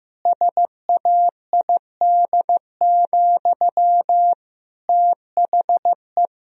Morse code, heard as a keyed tone, about 16 wpm, 700 Hz. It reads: SAID, THE
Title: Sentences from Top 1000 Words - Review Part 2 - 15wpm